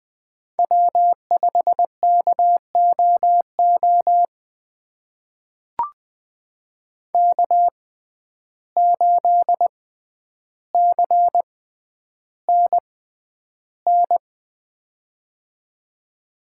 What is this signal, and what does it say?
Morse code, heard as a keyed tone, about 20 wpm, spaced as for 10 wpm, 700 Hz. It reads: W5KOO K8CNN
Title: USA Call Signs - 6x Character Spacing - 20wpm